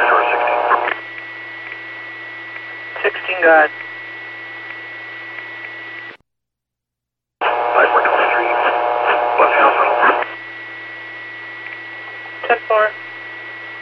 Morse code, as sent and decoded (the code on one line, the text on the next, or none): none